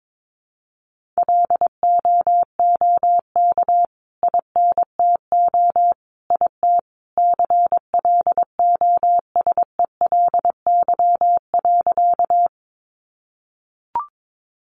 Morse code, heard as a keyed tone, about 22 wpm, 700 Hz. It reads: LOOK INTO IT CLOSELY.
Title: Sentences from Top 1400 Words 22wpm